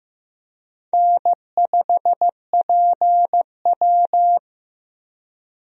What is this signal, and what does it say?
Morse code, heard as a keyed tone, about 15 wpm, 700 Hz. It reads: N5PW
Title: USA Call Signs - 4 Characters - Repeated 2x - 15wpm